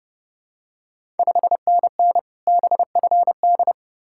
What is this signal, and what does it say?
Morse code, heard as a keyed tone, about 30 wpm, 700 Hz. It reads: 5NN BFD
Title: Ontario QSO Party Exchange 30wpm